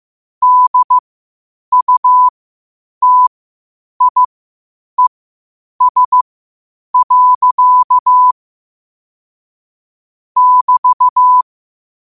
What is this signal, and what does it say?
Morse code, heard as a keyed tone, about 15 wpm, spaced as for 8 wpm, 1000 Hz. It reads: DUTIES. <BT>